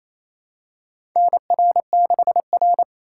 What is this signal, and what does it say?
Morse code, heard as a keyed tone, about 28 wpm, 690 Hz. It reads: NR6R
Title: USA Call Signs - 4 Characters - 2x Character Spacing - 28wpm